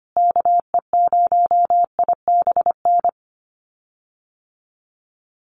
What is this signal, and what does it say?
Morse code, heard as a keyed tone, about 25 wpm, 700 Hz. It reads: KE0IBN
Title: USA Call Signs Repeated 3x 25wpm